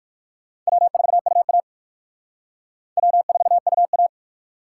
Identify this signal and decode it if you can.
Morse code, heard as a keyed tone, about 57 wpm, 700 Hz. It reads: W4UA W4UA